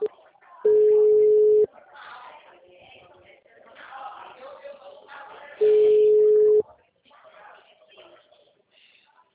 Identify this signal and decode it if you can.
Ringback tone (Europe, 425 Hz), with speech.
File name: out- -2020- - - .74624